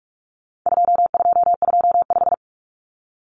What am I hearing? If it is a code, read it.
Morse code, heard as a keyed tone, about 44 wpm, 700 Hz. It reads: 2225
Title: Common Cut Numbers - 4-Digit 45wpm